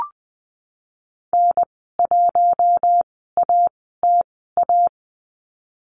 Morse code, heard as a keyed tone, about 20 wpm, spaced as for 15 wpm, 700 Hz. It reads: N1ATA
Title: 055 Lesson 10g - Letter r - Call Signs 20wpm